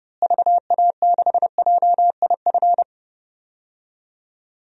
Morse code, heard as a keyed tone, about 30 wpm, 700 Hz. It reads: VA6JIF
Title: DXCC-100 Call Signs - 6 Characters - Repeated 3x - 30wpm